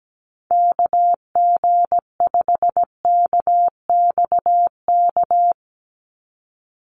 Morse code, heard as a keyed tone, about 17 wpm, 700 Hz. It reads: KG5KXK
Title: USA Call Signs 17wpm